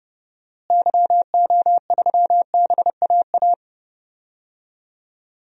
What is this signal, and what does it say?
Morse code, heard as a keyed tone, about 31 wpm, 700 Hz. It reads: YO3BAA
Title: DXCC-100 Call Signs - 6 Characters - Repeated 3x - 30wpm